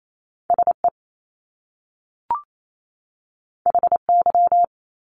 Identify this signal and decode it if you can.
Morse code, heard as a keyed tone, about 28 wpm, 700 Hz. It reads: SE HY
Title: Top Two Letter Combinations in Words 28wpm